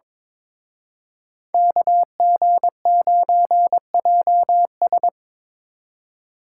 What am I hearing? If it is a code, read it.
Morse code, heard as a keyed tone, about 25 wpm, 700 Hz. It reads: KG9JS
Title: Speed Racing - USA Call Signs 22wpm